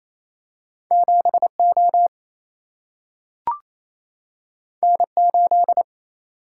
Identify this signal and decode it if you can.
Morse code, heard as a keyed tone, about 28 wpm, 700 Hz. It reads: 7 O N 8
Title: Single-Letter Single-Number 28wpm